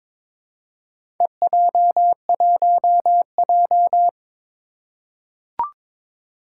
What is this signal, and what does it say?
Morse code, heard as a keyed tone, about 22 wpm, 700 Hz. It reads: EJ1J
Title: Four Challenging Letter-Number Combinations 22wpm